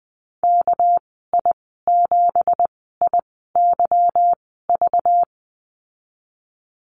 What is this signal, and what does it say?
Morse code, heard as a keyed tone, about 20 wpm, spaced as for 15 wpm, 700 Hz. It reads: KI7IYV
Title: 213 Lesson 34g - Number 8 - Call Signs 20wpm